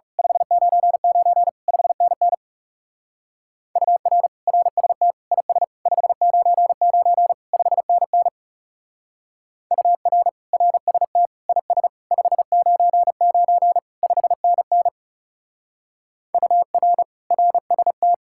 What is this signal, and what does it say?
Morse code, heard as a keyed tone, about 36 wpm, 700 Hz. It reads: 599 5NN UR RST IS 599 5NN UR RST IS 599 5NN UR RST
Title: Speed-Racing Common QSO Phrases - 30wpm